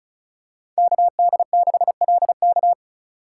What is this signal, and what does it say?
Morse code, heard as a keyed tone, about 34 wpm, 700 Hz. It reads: KD6LK
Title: USA Call Signs - 5 Characters - 4x Character Spacing - 35wpm